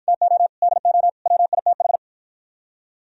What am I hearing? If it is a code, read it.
Morse code, heard as a keyed tone, about 52 wpm, 700 Hz. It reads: TO DO WITH